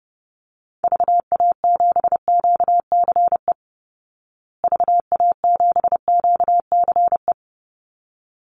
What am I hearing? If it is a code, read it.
Morse code, heard as a keyed tone, about 30 wpm, 700 Hz. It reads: VA7QCE VA7QCE
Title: Call Signs Repeated 3x - 30wpm